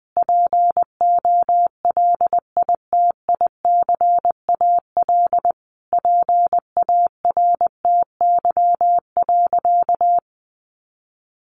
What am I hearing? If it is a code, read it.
Morse code, heard as a keyed tone, about 19 wpm, 700 Hz. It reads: POLITICAL PARTY.
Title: Sentences from Top 1400 Words 20wpm